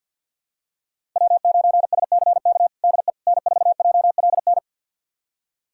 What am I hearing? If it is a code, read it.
Morse code, heard as a keyed tone, about 51 wpm, 700 Hz. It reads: W9SCK DE N4JLN
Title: USA Callsign DE Callsign - 5 Characters - 50wpm